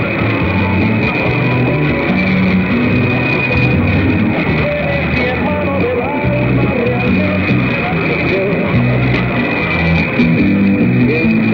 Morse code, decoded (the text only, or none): none